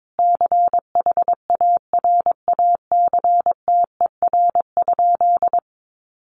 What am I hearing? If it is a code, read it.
Morse code, heard as a keyed tone, about 22 wpm, 700 Hz. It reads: CHARACTER?
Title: Sentences from Top 1200 Words 22wpm